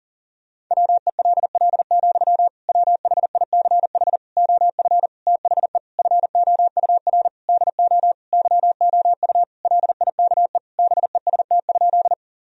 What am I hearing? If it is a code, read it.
Morse code, heard as a keyed tone, about 37 wpm, 700 Hz. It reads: WELL, WHICH OF THE FOUR DO YOU LIKE BEST?
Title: Sentences from Top 1000 Words - Review Part 1 - 40wpm